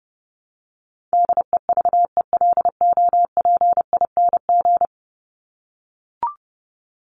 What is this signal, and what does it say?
Morse code, heard as a keyed tone, about 30 wpm, 700 Hz. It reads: DEVELOPING
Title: ING Suffix - Top 100 Words with Base Words 30wpm